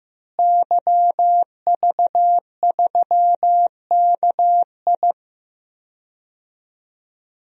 Morse code, heard as a keyed tone, about 15 wpm, 700 Hz. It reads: YV3KI
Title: DXCC-100 Call Signs - 5 Characters - Repeated 2x - 15wpm